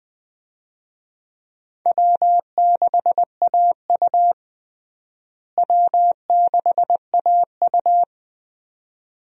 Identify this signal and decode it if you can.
Morse code, heard as a keyed tone, about 20 wpm, 700 Hz. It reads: W6AU W6AU
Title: USA Call Signs - 4 Characters - Repeated 2x - 20wpm